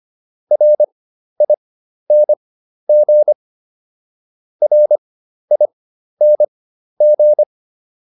Morse code, heard as a keyed tone, about 25 wpm, spaced as for 12 wpm, 600 Hz. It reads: RING RING